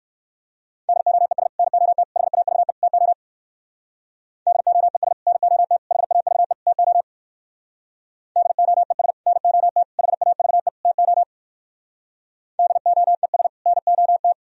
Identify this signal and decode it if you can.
Morse code, heard as a keyed tone, about 63 wpm, 700 Hz. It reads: DOES NOT HAVE TO DOES NOT HAVE TO DOES NOT HAVE TO DOES NOT